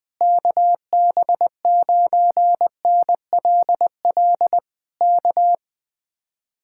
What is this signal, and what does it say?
Morse code, heard as a keyed tone, about 21 wpm, 700 Hz. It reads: KB9NLL K
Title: Speed-Racing Common QSO Phrases - 20wpm